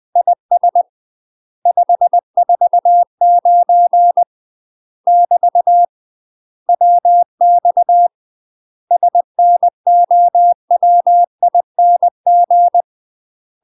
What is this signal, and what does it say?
Morse code, heard as a keyed tone, about 20 wpm, 700 Hz. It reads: IS 549 <BT> WX SNOWING
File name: UK MASTER QSOs TRAINING / 20WPM / UK MASTER QSOs 0052